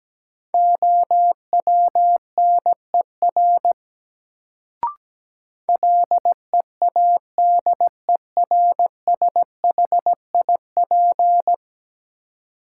Morse code, heard as a keyed tone, about 17 wpm, 700 Hz. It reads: OWNER LEADERSHIP
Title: Sentences from Top 1600 Words - New Word Drill Alt - 17wpm